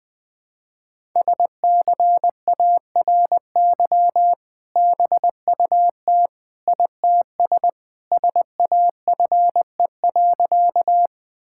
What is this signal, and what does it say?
Morse code, heard as a keyed tone, about 20 wpm, 700 Hz. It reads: SCARY BUT ITS SAFE.
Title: Sentences with 2-Word Phrases - Spoken Beginning - 20wpm